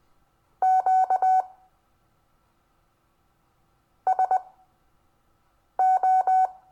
Morse code, heard as a keyed tone, about 21 wpm, spaced as for 4 wpm, 750 Hz. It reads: QSO
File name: cw / Lesson22 REVIEW